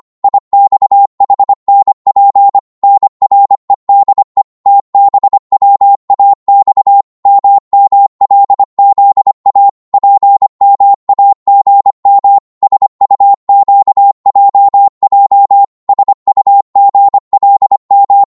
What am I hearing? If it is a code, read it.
Morse code, heard as a keyed tone, about 25 wpm, 800 Hz. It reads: IXHNP NREDE TBWAX MMLZA PMAGM SUQJJ SUGLM
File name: Morse / ltr-25wpm